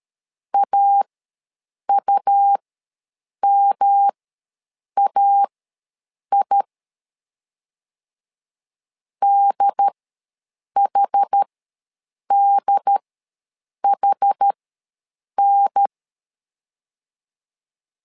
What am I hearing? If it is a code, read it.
Morse code, heard as a keyed tone, about 13 wpm, spaced as for 7 wpm, 800 Hz. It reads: AUMAI DHDHN
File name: LEZIONI / 3 MWRO / 3 M ws 12wpm 1745